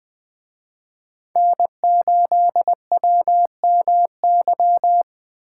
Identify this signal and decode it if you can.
Morse code, heard as a keyed tone, about 20 wpm, 700 Hz. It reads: N8WMY
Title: Speed Racing - USA 1x3 Call Signs 17wpm